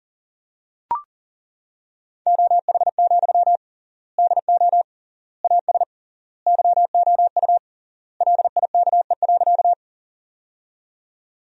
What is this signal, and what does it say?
Morse code, heard as a keyed tone, about 40 wpm, 700 Hz. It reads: OH, DO AS YOU LIKE.
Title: 3x Word-Spacing - Sentences from Top 500 Words - 40wpm